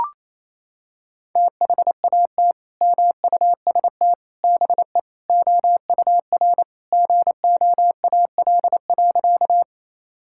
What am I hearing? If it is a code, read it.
Morse code, heard as a keyed tone, about 28 wpm, 700 Hz. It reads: THAT MUST BE OUR GOAL.
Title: Sentences from Top 1000 Words - Review Part 5 - 28wpm